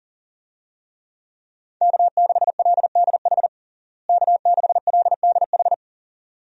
Morse code, heard as a keyed tone, about 40 wpm, 700 Hz. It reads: K6LDH K6LDH
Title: USA Call Signs - 5 Characters - Repeated 2x - 40wpm